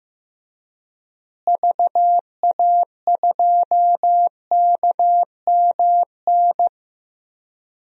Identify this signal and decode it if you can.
Morse code, heard as a keyed tone, about 15 wpm, 700 Hz. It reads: VA2KMN
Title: DXCC-100 Call Signs - 6 Characters - Repeated 3x - 15wpm